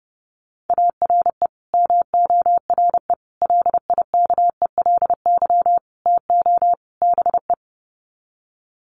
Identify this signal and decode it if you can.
Morse code, heard as a keyed tone, about 30 wpm, 700 Hz. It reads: ARE MORE LIKELY TO BE
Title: Sets of 5 Words 30wpm